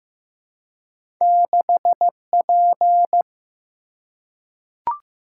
Morse code, heard as a keyed tone, about 15 wpm, 700 Hz. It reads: 6P
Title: Single-Letter Single-Number 15wpm